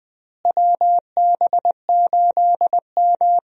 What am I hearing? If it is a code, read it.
Morse code, heard as a keyed tone, about 20 wpm, 700 Hz. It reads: WB8M
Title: USA Call Signs - 4 Characters - Repeated 3x - 20wpm